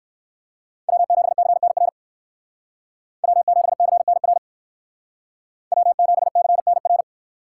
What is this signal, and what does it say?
Morse code, heard as a keyed tone, about 62 wpm, 700 Hz. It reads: W7CNR W7CNR W7CNR